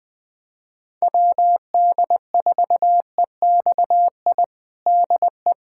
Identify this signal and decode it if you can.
Morse code, heard as a keyed tone, about 21 wpm, 700 Hz. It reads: WD4EXI DE
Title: USA Callsign DE Callsign - 6 Characters - 20wpm